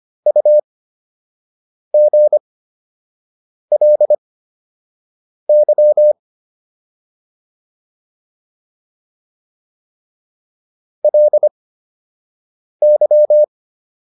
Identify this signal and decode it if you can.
Morse code, heard as a keyed tone, about 25 wpm, spaced as for 6 wpm, 600 Hz. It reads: UGLY LY